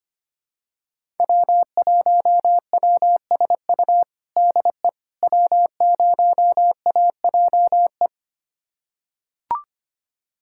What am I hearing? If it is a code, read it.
Morse code, heard as a keyed tone, about 25 wpm, 700 Hz. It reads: W1WSU DE W0AJE
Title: USA Callsign DE Callsign - 5 Characters - 25wpm